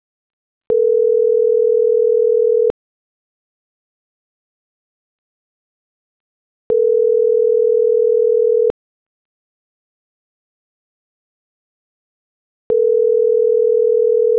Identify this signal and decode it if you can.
Ringback tone (North America), 440 + 480 Hz.